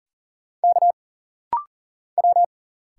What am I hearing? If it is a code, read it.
Morse code, heard as a keyed tone, about 40 wpm, 700 Hz. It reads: K W
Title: Single Letter-Number - Rapid-Fire 40wpm